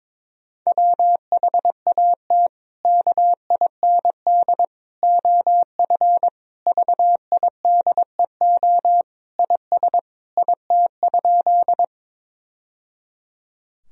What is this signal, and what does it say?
Morse code, heard as a keyed tone, about 22 wpm, 700 Hz. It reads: WHAT KIND OF VIDEO IS IT?
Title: Sentences from Top 1000 Words - Review Part 4 - 22wpm